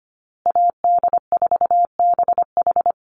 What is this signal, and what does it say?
Morse code, heard as a keyed tone, about 25 wpm, 700 Hz. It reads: AD4BH
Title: USA Call Signs Repeated 2x 25wpm